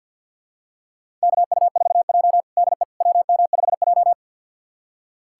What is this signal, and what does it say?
Morse code, heard as a keyed tone, about 46 wpm, 700 Hz. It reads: KU4J DE WM5J
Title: USA Callsign DE Callsign - 4 Characters - 50wpm